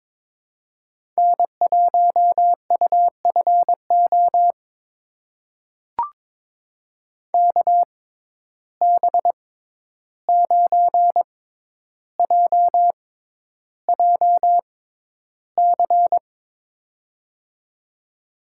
Morse code, heard as a keyed tone, about 22 wpm, spaced as for 11 wpm, 700 Hz. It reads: N1UFO KB9JJC
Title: USA Call Signs - 6x Character Spacing - 22wpm